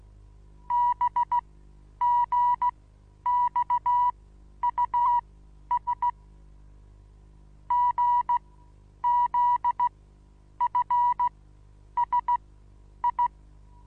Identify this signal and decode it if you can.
Morse code, heard as a keyed tone, about 16 wpm, spaced as for 9 wpm, 970 Hz. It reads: BGXUS GZFSI
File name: Onlinelehrgang / morsen / DJ4UF Morsen25